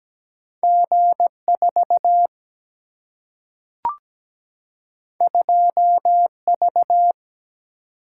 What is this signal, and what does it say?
Morse code, heard as a keyed tone, about 17 wpm, 700 Hz. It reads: G 4 2 V
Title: Single-Letter Single-Number 17wpm